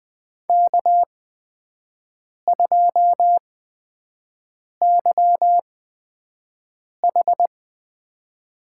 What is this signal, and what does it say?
Morse code, heard as a keyed tone, about 20 wpm, spaced as for 5 wpm, 700 Hz. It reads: K2YH
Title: USA Call Signs - 5 Characters - 8x Character Spacing - 20wpm